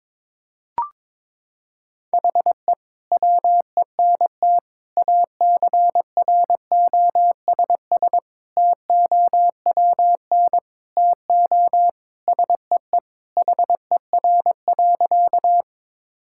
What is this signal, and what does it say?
Morse code, heard as a keyed tone, about 21 wpm, 700 Hz. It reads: HE WENT ACROSS TOWN TO SEE HER.
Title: Sentences from Top 1000 Words - Review Part 5 - 22wpm